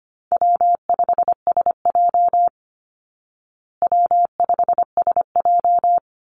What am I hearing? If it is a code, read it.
Morse code, heard as a keyed tone, about 25 wpm, 700 Hz. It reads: W5SJ W5SJ
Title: Easy Call Signs Repeated 2x - 25wpm